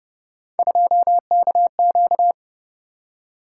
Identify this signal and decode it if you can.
Morse code, heard as a keyed tone, about 30 wpm, 700 Hz. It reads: 2KQ
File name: Speed-Racing - Three Letter-Number Combinations - 25wpm